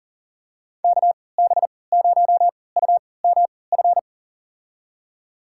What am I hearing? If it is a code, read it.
Morse code, heard as a keyed tone, about 37 wpm, spaced as for 22 wpm, 700 Hz. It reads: KB0UMF
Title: USA Call Signs - 6 Characters - 3x Character Spacing - 40wpm